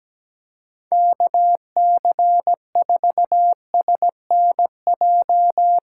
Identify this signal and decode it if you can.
Morse code, heard as a keyed tone, about 17 wpm, 700 Hz. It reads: KC4SNJ
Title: USA Call Signs - 6 Characters - 6x Character Spacing - 17wpm